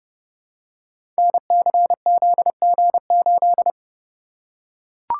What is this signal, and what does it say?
Morse code, heard as a keyed tone, about 31 wpm, 700 Hz. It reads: NCZG8
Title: Code Groups - 5 Letter:Numbers - 30wpm